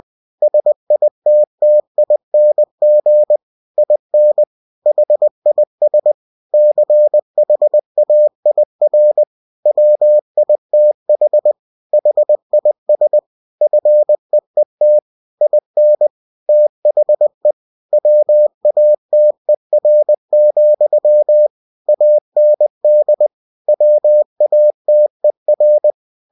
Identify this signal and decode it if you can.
Morse code, heard as a keyed tone, about 20 wpm, 600 Hz. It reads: SITTING IN HIS CHAIR WITH HIS FEET IN THE WATER, AND WATER
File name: cw-books / winnie-the-pooh / winnie-20wpm-page-0094